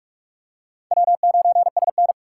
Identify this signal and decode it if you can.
Morse code, heard as a keyed tone, about 48 wpm, 700 Hz. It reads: W9SN
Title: USA Call Signs - 4x Character Spacing - 45wpm